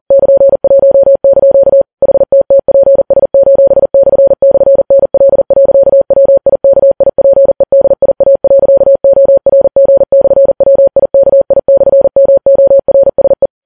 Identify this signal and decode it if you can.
Morse code, heard as a keyed tone, about 40 wpm, 550 Hz. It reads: <KN>1) HTTPS://NL.WIKIPEDIA.ORG/WIKI/MORSE
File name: Files Web tls / Morse on9fv / morse40kps